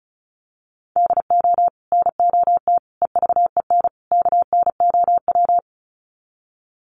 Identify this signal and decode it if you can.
Morse code, heard as a keyed tone, about 34 wpm, 700 Hz. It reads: DO NOT EVEN KNOW
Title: Sets of 4 Words 35wpm